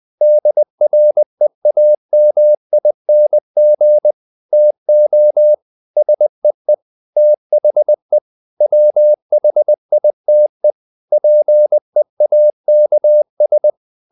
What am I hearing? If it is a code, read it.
Morse code, heard as a keyed tone, about 20 wpm, 600 Hz. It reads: DREAMING TO SEE THE WHITE PEAKS
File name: the old man and the sea / the old man and the sea part 1 20wpm